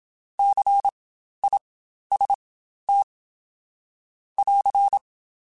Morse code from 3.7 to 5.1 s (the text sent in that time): <AR>